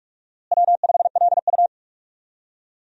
Morse code, heard as a keyed tone, about 46 wpm, 700 Hz. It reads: W5LU
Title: USA Call Signs - 4 Characters - Repeated 3x - 45wpm